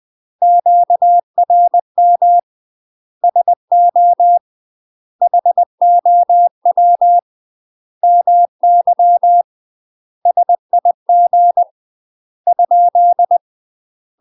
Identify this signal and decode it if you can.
Morse code, heard as a keyed tone, about 20 wpm, 700 Hz. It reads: QRM SO HOW MY SIG ?
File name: UK MASTER QSOs TRAINING / 20WPM / UK MASTER QSOs 0029